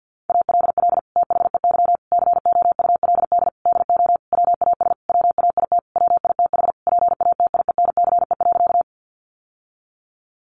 Morse code, heard as a keyed tone, about 48 wpm, 700 Hz. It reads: ALL THEY COULD DO WAS WAIT WITH PATIENCE.
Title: Sentences from Top 1300 Words 50wpm